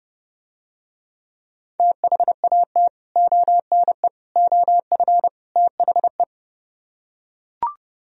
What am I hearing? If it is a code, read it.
Morse code, heard as a keyed tone, about 30 wpm, 700 Hz. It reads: THAT ONE OF THE